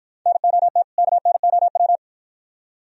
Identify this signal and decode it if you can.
Morse code, heard as a keyed tone, about 52 wpm, 700 Hz. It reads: NOT KNOW